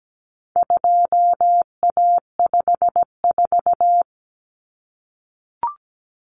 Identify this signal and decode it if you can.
Morse code, heard as a keyed tone, about 17 wpm, 700 Hz. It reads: 2A54
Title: Common Cut Numbers - 4-Digit 17wpm